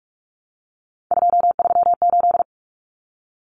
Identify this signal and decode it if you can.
Morse code, heard as a keyed tone, about 44 wpm, 700 Hz. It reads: 238
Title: Numbers - 3-Digit 45wpm